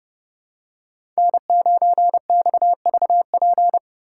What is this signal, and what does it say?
Morse code, heard as a keyed tone, about 29 wpm, 700 Hz. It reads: N9XVP